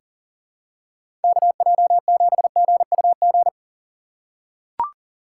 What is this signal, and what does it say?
Morse code, heard as a keyed tone, about 40 wpm, 700 Hz. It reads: KJ7GUG
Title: USA Call Signs - 6 Characters - 6x Character Spacing - 40wpm